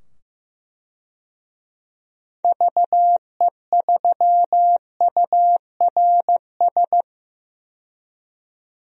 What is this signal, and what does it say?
Morse code, heard as a keyed tone, about 15 wpm, 700 Hz. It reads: VE3URS
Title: DXCC-100 Call Signs - 6 Characters - 15wpm